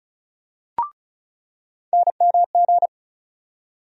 Morse code, heard as a keyed tone, about 36 wpm, 700 Hz. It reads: NMG
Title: USA Call Sign Postfixes - 3 Letters 35wpm